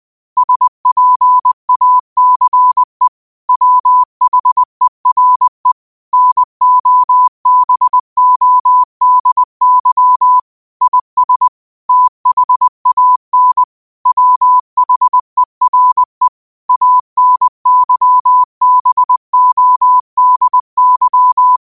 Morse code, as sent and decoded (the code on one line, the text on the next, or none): ... .--. .- -.-. / .-- .... .-. / -. --- -... --- -.. -.-- / .. ... / - .... .- -. / .-- .... .-. / .- -. -.-- -... --- -.. -.--
SPACE WHERE NOBODY IS THAN WHERE ANYBODY